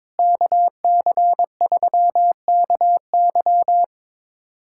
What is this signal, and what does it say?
Morse code, heard as a keyed tone, about 22 wpm, 700 Hz. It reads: KC3KY
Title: USA Callsign DE Callsign 22wpm